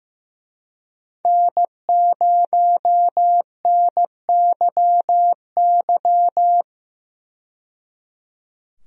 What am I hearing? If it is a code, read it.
Morse code, heard as a keyed tone, about 15 wpm, 700 Hz. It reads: N0NYY
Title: USA Call Signs - 5 Characters - 15wpm